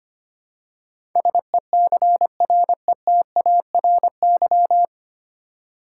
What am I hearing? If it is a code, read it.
Morse code, heard as a keyed tone, about 25 wpm, 700 Hz. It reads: SECRETARY
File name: practice / 25wpm / 9 Letter Words - First 3 Letters Given - 25wpm